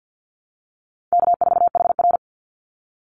Morse code, heard as a keyed tone, about 50 wpm, 700 Hz. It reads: K4HR
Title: Call Signs Repeated 2x - 50wpm